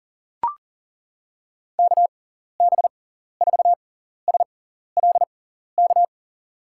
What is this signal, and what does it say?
Morse code, heard as a keyed tone, about 38 wpm, spaced as for 14 wpm, 700 Hz. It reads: KB4SLK
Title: USA Call Signs - 6 Characters - 6x Character Spacing - 40wpm